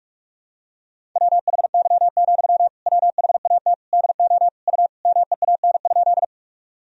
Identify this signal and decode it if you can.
Morse code, heard as a keyed tone, about 42 wpm, 700 Hz. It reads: WHY, WHAT DO U MEAN?
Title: Sentences from Top 200 Words and Abbreviations 45wpm